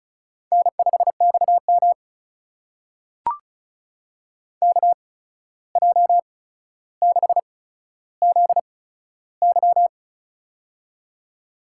Morse code, heard as a keyed tone, about 36 wpm, spaced as for 14 wpm, 700 Hz. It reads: N5XM KJ6ZY
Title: USA Call Signs - 8x Character Spacing - 35wpm